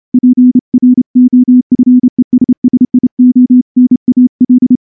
Morse code, heard as a keyed tone, about 28 wpm, 250 Hz. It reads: PROFESSIONAL
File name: morse-training-30WPM